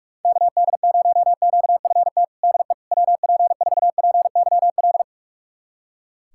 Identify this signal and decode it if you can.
Morse code, heard as a keyed tone, about 46 wpm, 700 Hz. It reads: KD0QFT DE WP4PYL
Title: USA Callsign DE Callsign - 6 Characters - 45wpm